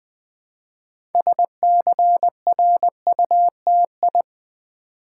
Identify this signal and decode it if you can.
Morse code, heard as a keyed tone, about 20 wpm, 700 Hz. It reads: SCRUTI
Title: Build A Word - 2 Letters At A Time - 8-Letter Words - 20wpm